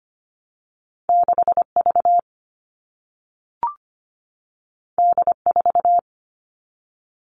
Morse code, heard as a keyed tone, about 25 wpm, 700 Hz. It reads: 6V D4
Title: All Cut Numbers - 2-Digit 25wpm